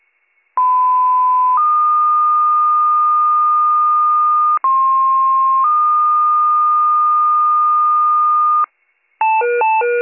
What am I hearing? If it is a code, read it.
Two-tone page, heard as a one-second tone, then a three-second tone, about 1010 then 1220 Hz, about 1060 then 1220 Hz. It is heard from 0.6 to 8.6 s.